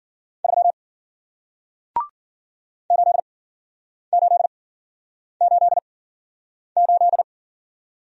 Morse code, heard as a keyed tone, about 51 wpm, spaced as for 9 wpm, 700 Hz. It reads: 3 8888